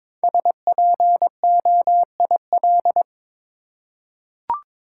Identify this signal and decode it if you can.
Morse code, heard as a keyed tone, about 23 wpm, 700 Hz. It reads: SPOIL
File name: 5 Letter Words - First 3 Letters Given - 22wpm